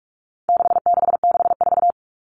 Morse code, heard as a keyed tone, about 44 wpm, 700 Hz. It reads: 6664
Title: Numbers - 4-Digit 45wpm